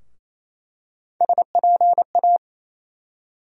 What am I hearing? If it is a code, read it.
Morse code, heard as a keyed tone, about 28 wpm, 700 Hz. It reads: SPA
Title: USA Call Sign Postfixes - 3 Letters 28wpm